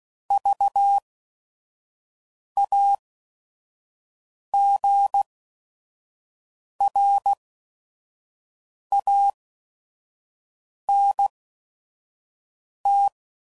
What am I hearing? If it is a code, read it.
Morse code, heard as a keyed tone, about 16 wpm, spaced as for 5 wpm, 800 Hz. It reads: VAGRANT
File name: Prefixes / By First Letter / Vs